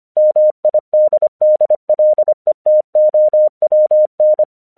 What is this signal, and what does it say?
Morse code, heard as a keyed tone, about 25 wpm, 610 Hz. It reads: MIDDLETOWN
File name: 1 English 25wpm / AQC06